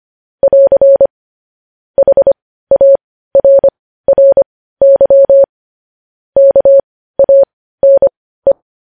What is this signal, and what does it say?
Morse code, heard as a keyed tone, about 25 wpm, spaced as for 15 wpm, 550 Hz. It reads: <AR> HARRY KANE
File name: podcast / News headlines-1x-25x15